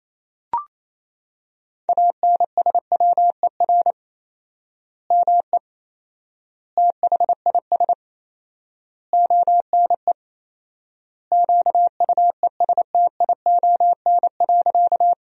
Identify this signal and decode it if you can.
Morse code, heard as a keyed tone, about 28 wpm, 700 Hz. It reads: ANSWER ME THIS ONE QUESTION.
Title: 4x Word-Spacing - Sentences from Top 400 Words - 28wpm